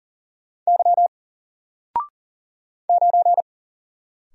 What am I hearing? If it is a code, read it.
Morse code, heard as a keyed tone, about 40 wpm, 700 Hz. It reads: Y 9
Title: Single Character (Letter:Number:Punctuation:Prosign) 40wpm